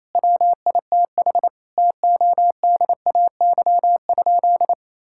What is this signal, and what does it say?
Morse code, heard as a keyed tone, about 28 wpm, 700 Hz. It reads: WITH TODAY?
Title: Sentences from Top 1000 Words - Review Part 6 - 28wpm